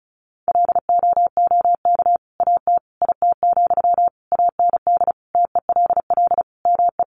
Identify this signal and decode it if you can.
Morse code, heard as a keyed tone, about 34 wpm, 700 Hz. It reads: LOOK AT IT, AND TELL ME
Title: Sentences from Top 200 Words 35wpm